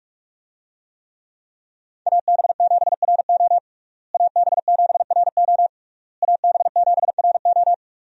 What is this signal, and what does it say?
Morse code, heard as a keyed tone, about 46 wpm, 700 Hz. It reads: AB7RO AB7RO AB7RO
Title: USA Call Signs - 5 Characters - Repeated 3x - 45wpm